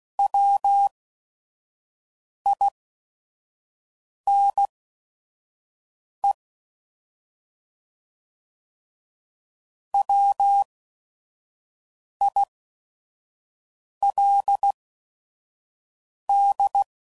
Morse code, heard as a keyed tone, about 16 wpm, spaced as for 5 wpm, 800 Hz. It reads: WINE WILD
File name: Prefixes / By First Letter / Ws